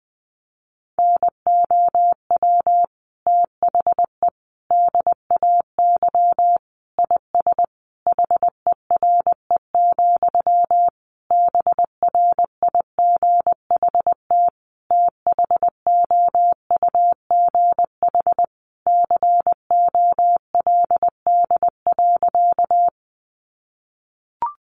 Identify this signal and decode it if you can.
Morse code, heard as a keyed tone, about 20 wpm, 700 Hz. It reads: NOW THE DAY IS HERE, BRIGHT THOUGH COLD.
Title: Sentences from Top 1000 Words 20wpm